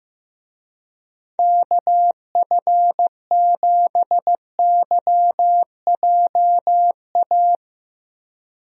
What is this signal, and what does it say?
Morse code, heard as a keyed tone, about 15 wpm, 700 Hz. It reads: KF7YJA
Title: USA Call Signs - 6 Characters - Repeated 3x - 15wpm